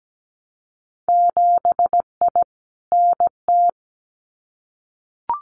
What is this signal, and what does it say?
Morse code, heard as a keyed tone, about 17 wpm, 700 Hz. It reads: 7I NT
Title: Winter Field Day Exchanges Encore 17wpm